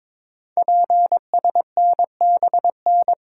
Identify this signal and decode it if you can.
Morse code, heard as a keyed tone, about 22 wpm, 700 Hz. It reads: PSNBN
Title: Code Groups - 5 Letters - 22wpm